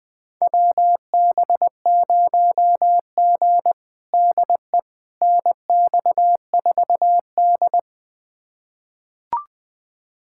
Morse code, heard as a keyed tone, about 20 wpm, 700 Hz. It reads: WB0G DE NX4D
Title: USA Callsign DE Callsign - 4 Characters - 20wpm